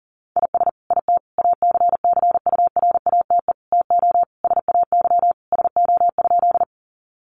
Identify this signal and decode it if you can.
Morse code, heard as a keyed tone, about 40 wpm, 700 Hz. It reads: IS IT ACCURATE TO SAY SO?
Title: Sentences from Top 1200 Words 40wpm